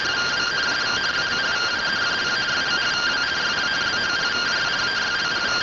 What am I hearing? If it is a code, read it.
Two-tone teleprinter signal, about 170 Hz shift.